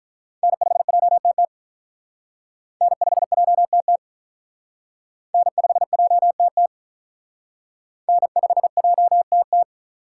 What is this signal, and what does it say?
Morse code, heard as a keyed tone, about 46 wpm, 700 Hz. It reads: N5JTT N5JTT N5JTT N5JTT